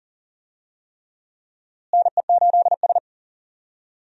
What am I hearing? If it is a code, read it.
Morse code, heard as a keyed tone, about 40 wpm, 700 Hz. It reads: NE8S
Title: USA Call Signs - 4 Characters - Repeated 3x - 40wpm